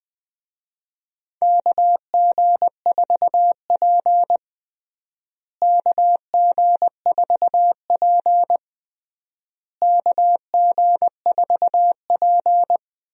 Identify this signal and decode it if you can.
Morse code, heard as a keyed tone, about 20 wpm, 700 Hz. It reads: KG4P KG4P KG4P
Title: USA Call Signs - 4 Characters - Repeated 3x - 20wpm